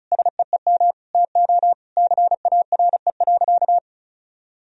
Morse code, heard as a keyed tone, about 34 wpm, 700 Hz. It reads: SEEM TO CARE.
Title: Sentences from Top 1000 Words - Review Part 4 - 35wpm